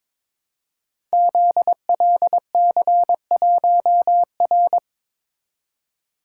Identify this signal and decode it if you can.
Morse code, heard as a keyed tone, about 22 wpm, 700 Hz. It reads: ZLC1R
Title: Code Groups - 5 Letter:Numbers - 22wpm